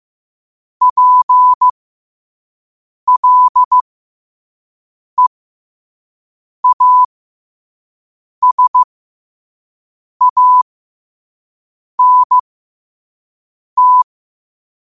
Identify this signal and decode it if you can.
Morse code, heard as a keyed tone, about 15 wpm, spaced as for 5 wpm, 1000 Hz. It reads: PLEASANT
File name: qotdfiles / 5wpm